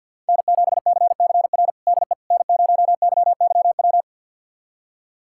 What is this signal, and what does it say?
Morse code, heard as a keyed tone, about 50 wpm, 700 Hz. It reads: N7CCR DE N0YYW
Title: USA Callsign DE Callsign - 5 Characters - 50wpm